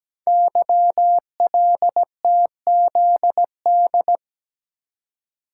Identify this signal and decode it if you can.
Morse code, heard as a keyed tone, about 17 wpm, 700 Hz. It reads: YLTZD
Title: Code Groups - 5 Letter:Numbers - 17wpm